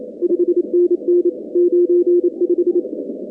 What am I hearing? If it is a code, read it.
Morse code, heard as a keyed tone, about 28 wpm, 350 Hz. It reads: 5NN 95
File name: contests / 2019-cw-wpx-kt5j-k5pi- -2330 0-45